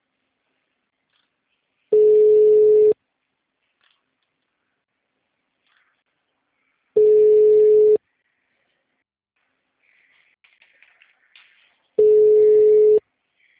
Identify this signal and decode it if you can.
Ringback tone (Europe, 425 Hz).
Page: Guarapuava - PR